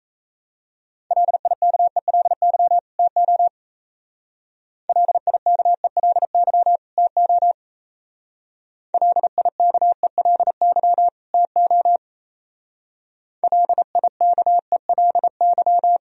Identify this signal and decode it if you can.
Morse code, heard as a keyed tone, about 35 wpm, 700 Hz. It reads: LIKELY TO LIKELY TO LIKELY TO LIKELY
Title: Speed-Racing - Sets of 2 Words 28wpm